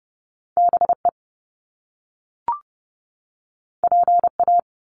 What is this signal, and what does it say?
Morse code, heard as a keyed tone, about 30 wpm, 700 Hz. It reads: BE PA
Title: Top Two Letter Combinations in Words 30wpm